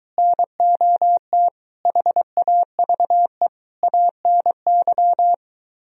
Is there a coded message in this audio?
Yes: Morse code from 0.2 to 5.3 s, heard as a keyed tone, about 23 wpm, 700 Hz. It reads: NOT HAVE ANY